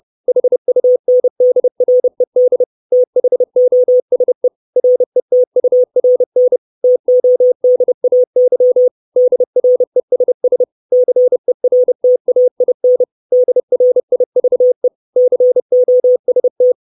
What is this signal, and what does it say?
Morse code, heard as a keyed tone, about 30 wpm, 500 Hz. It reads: HUNDRED THOSE RETURN TODAY DRESS CERTAIN DRIVE COST
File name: 632 Words 30WPM